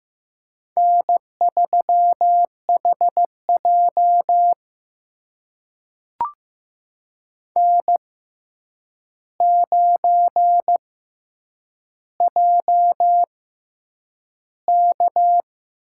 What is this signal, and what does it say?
Morse code, heard as a keyed tone, about 15 wpm, spaced as for 8 wpm, 700 Hz. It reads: N3HJ N9JK
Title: USA Call Signs - 4 Characters - 6x Character Spacing - 15wpm